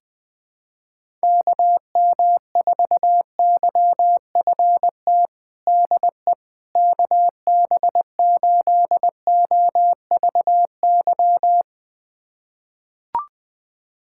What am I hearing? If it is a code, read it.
Morse code, heard as a keyed tone, about 20 wpm, 700 Hz. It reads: KM4YFT DE KB8OVY
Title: USA Callsign DE Callsign - 6 Characters - 20wpm